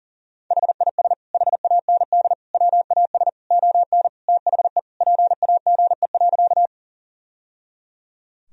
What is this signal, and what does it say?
Morse code, heard as a keyed tone, about 37 wpm, 700 Hz. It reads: HIS HAND WAS ON THE PAGE.
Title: Sentences from Top 1000 Words - Review Part 1 - 40wpm